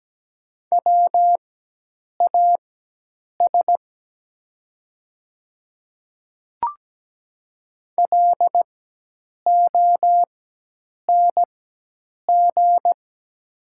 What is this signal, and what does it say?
Morse code, heard as a keyed tone, about 17 wpm, spaced as for 8 wpm, 700 Hz. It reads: WAS LONG
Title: 4x Character Spacing - Top 200 Words - 17wpm